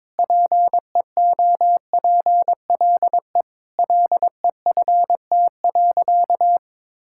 Morse code, heard as a keyed tone, about 22 wpm, 700 Hz. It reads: PEOPLE LEFT.
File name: practice / 22wpm / Sentences with 4-Word Phrases - 22wpm